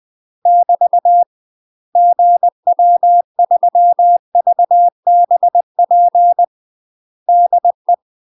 Morse code, heard as a keyed tone, about 20 wpm, 700 Hz. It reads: <BT> GW3VBP DE
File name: UK MASTER QSOs TRAINING / 20WPM / UK MASTER QSOs 0023